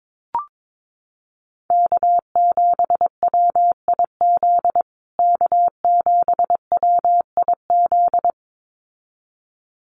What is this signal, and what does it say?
Morse code, heard as a keyed tone, about 22 wpm, 700 Hz. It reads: K7WIZ K7WIZ
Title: USA Call Signs Repeated 2x 22wpm